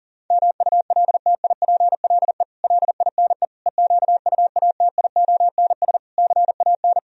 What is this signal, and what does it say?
Morse code, heard as a keyed tone, about 37 wpm, 700 Hz. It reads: MULTIPLE LINE EQUATIONS CAN
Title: Speed-Racing - Sets of 5 Words 30wpm